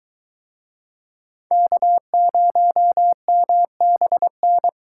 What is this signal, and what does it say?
Morse code, heard as a keyed tone, about 23 wpm, 700 Hz. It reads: K0MBN